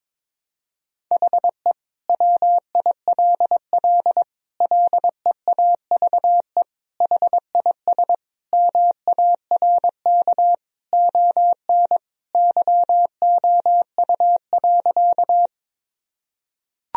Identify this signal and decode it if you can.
Morse code, heard as a keyed tone, about 22 wpm, 700 Hz. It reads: HE WILL LEAVE HIS MARK ON YOU.
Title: Sentences from Top 1000 Words - Review Part 1 - 22wpm